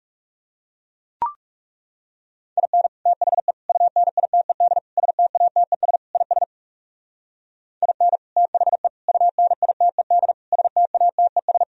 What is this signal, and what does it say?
Morse code, heard as a keyed tone, about 37 wpm, 700 Hz. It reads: IN THE UNITED STATES IS IN THE UNITED STATES
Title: Speed-Racing - Sets of 5 Words 30wpm